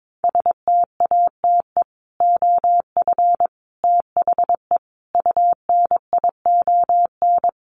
Morse code, heard as a keyed tone, about 22 wpm, 700 Hz. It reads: STATE OF THE UNION
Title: Sets of 5 Words Encore 22wpm